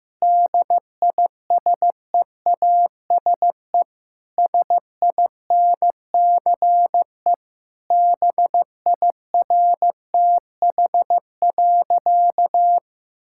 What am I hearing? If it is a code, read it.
Morse code, heard as a keyed tone, about 16 wpm, 700 Hz. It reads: DISEASE SINCE BIRTH.
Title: Sentences from Top 1000 Words - Review Part 6 - 15wpm